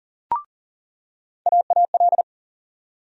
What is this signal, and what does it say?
Morse code, heard as a keyed tone, about 40 wpm, 700 Hz. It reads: AAL
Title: USA Call Sign Postfixes - 3 Letters 40wpm